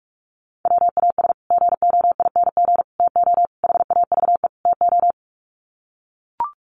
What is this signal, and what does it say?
Morse code, heard as a keyed tone, about 44 wpm, 700 Hz. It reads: WAS GOING TO HAVE TO
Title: Sets of 5 Words Encore 45wpm